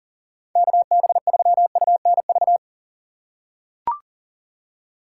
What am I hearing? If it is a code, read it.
Morse code, heard as a keyed tone, about 40 wpm, 700 Hz. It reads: KB3UNV
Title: USA Call Signs - 6 Characters - 4x Character Spacing - 40wpm